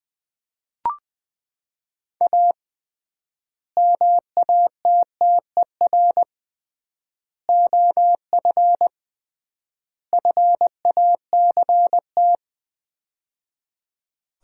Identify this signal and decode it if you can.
Morse code, heard as a keyed tone, about 20 wpm, 700 Hz. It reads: A MATTER OF FACT
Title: 3x Word-Spacing - Sets of 4 Words - 20wpm